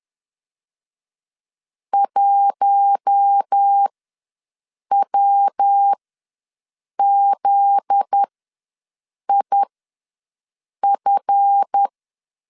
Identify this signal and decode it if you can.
Morse code, heard as a keyed tone, about 10 wpm, spaced as for 6 wpm, 800 Hz. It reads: 1WZIF
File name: LEZIONI / 7 GF9Z / 7 GF9Z ws 10wpm 3919